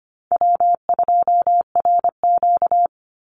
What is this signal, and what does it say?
Morse code, heard as a keyed tone, about 25 wpm, 700 Hz. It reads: W2RQ
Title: Call Signs Repeated 3x - 25wpm